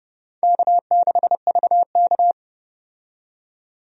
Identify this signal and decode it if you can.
Morse code, heard as a keyed tone, about 30 wpm, 700 Hz. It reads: K6VK
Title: USA Call Signs - 4 Characters - 30wpm